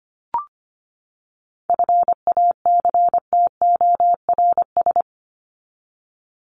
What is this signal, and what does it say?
Morse code, heard as a keyed tone, about 25 wpm, 700 Hz. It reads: FACTORS
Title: ES:S Suffix - Top 100 Words with Base Words 25wpm